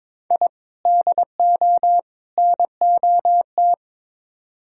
Morse code, heard as a keyed tone, about 22 wpm, 700 Hz. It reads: I DO NOT
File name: practice / 22wpm / Build A Phrase - 5-Word Phrases - 22wpm